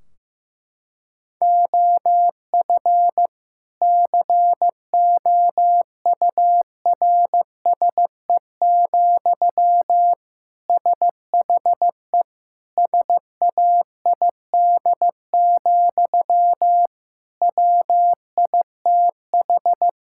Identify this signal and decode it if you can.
Morse code, heard as a keyed tone, about 15 wpm, 700 Hz. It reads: OF COURSE, SHE SAID, WITH
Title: Sentences from Top 1600 Words 15wpm